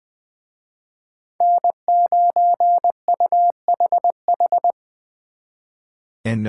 Morse code, heard as a keyed tone, about 24 wpm, 700 Hz. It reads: N9UHH
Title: Speed Racing - USA Call Signs 20wpm